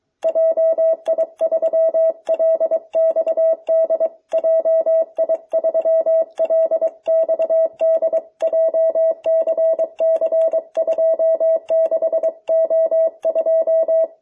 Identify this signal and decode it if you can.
Morse code, heard as a keyed tone, about 22 wpm, 660 Hz. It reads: JI3LXD JI3LXD JCC26O2